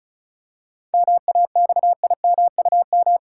Morse code, heard as a keyed tone, about 34 wpm, 700 Hz. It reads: MAXIMUM
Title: Seven Letter Words - 35wpm